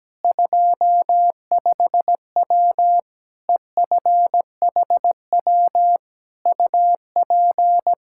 Morse code, heard as a keyed tone, about 17 wpm, 700 Hz. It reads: 25W EFHW UP
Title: Practice QSO - Exchange 2 - Info - 17wpm